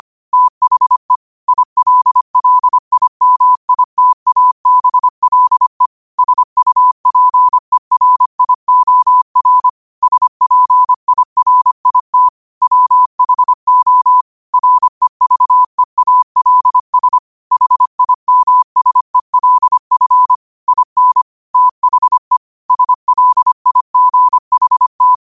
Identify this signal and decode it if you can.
Morse code, heard as a keyed tone, about 25 wpm, 1000 Hz. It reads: THE ILLIMITABLE SUPERIOR SPIRIT WHO REVEALS HIMSELF IN THE SLIGHT
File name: qotdfiles / 25wpm